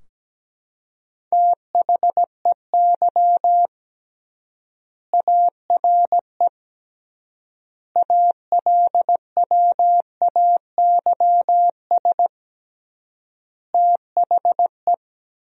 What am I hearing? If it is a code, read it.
Morse code, heard as a keyed tone, about 17 wpm, 700 Hz. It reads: THEY ARE ALWAYS THE
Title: 3x Word-Spacing - Sentences from Top 300 Words - 17wpm